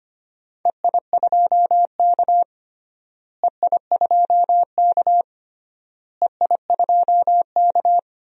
Morse code, heard as a keyed tone, about 25 wpm, 700 Hz. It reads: EI2K EI2K EI2K
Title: DXCC-100 Call Signs - 4 Characters - Repeated 3x - 25wpm